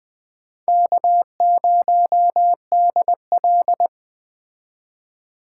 Morse code, heard as a keyed tone, about 21 wpm, 700 Hz. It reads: K0DL
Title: Speed Racing - USA 1x2 Call Signs 17wpm